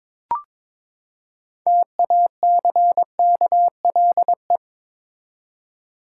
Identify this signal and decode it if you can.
Morse code, heard as a keyed tone, about 22 wpm, 700 Hz. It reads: TACKLE
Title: Six Letter Words - 22wpm